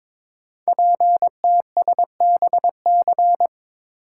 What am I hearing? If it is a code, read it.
Morse code, heard as a keyed tone, about 23 wpm, 700 Hz. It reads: PTSBC
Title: Code Groups - 5 Letters - 22wpm